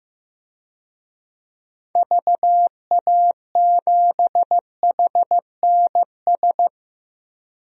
Morse code, heard as a keyed tone, about 15 wpm, 700 Hz. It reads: VA7HNS
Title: DXCC-100 Call Signs - 6 Characters - 15wpm